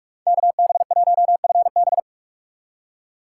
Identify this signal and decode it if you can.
Morse code, heard as a keyed tone, about 44 wpm, 700 Hz. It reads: KB1FB
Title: USA Call Signs - 5 Characters - 2x Character Spacing - 45wpm